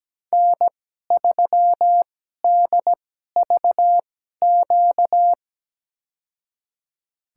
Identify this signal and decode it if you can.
Morse code, heard as a keyed tone, about 17 wpm, spaced as for 12 wpm, 700 Hz. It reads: N3DVQ
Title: USA Call Signs - 2x Character Spacing - 17wpm